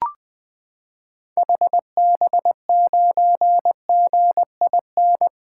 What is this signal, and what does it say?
Morse code, heard as a keyed tone, about 20 wpm, 700 Hz. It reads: HB9GIN
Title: 2023 SOTA Activator Call Signs - 20wpm